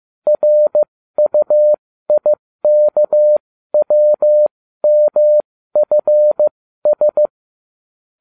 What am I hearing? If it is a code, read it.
Morse code, heard as a keyed tone, about 15 wpm, spaced as for 13 wpm, 600 Hz. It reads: RUIKWMFS